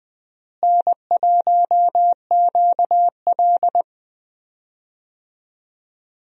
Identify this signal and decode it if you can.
Morse code, heard as a keyed tone, about 21 wpm, 700 Hz. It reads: N1QL
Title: USA Call Signs - 4 Characters - Repeated 2x - 20wpm